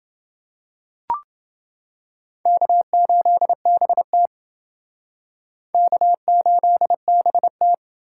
Morse code, heard as a keyed tone, about 28 wpm, 700 Hz. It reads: K8BT K8BT